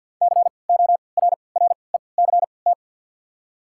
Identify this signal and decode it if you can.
Morse code, heard as a keyed tone, about 53 wpm, spaced as for 29 wpm, 700 Hz. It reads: CORRECT
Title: 3x Character Spacing - Top 500 Words - 50wpm